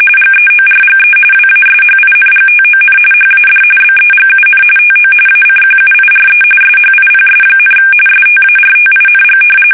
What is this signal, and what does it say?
RTTY, about 45.45 baud. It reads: GEORGE WASHINGTON HAD BEGUN TO ACQUIRE SHIPS, STARTING
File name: fsk850 / usnavy1